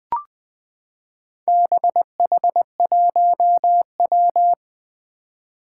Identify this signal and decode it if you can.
Morse code, heard as a keyed tone, about 20 wpm, 700 Hz. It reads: BH1W
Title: Four Challenging Letter-Number Combinations 20wpm